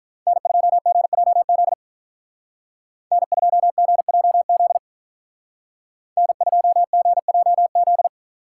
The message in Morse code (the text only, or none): N2GJZ N2GJZ N2GJZ